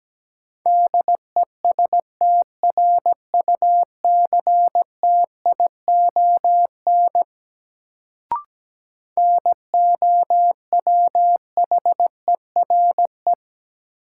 Morse code, heard as a keyed tone, about 17 wpm, 700 Hz. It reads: DESTRUCTION NOWHERE
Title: Sets of 5 Words Encore - New Word Drill - 17wpm